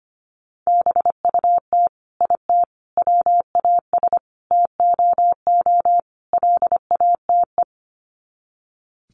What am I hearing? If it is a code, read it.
Morse code, heard as a keyed tone, about 25 wpm, 700 Hz. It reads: BUT IT WAS TOO LATE
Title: Sets of 5 Words Encore2 25wpm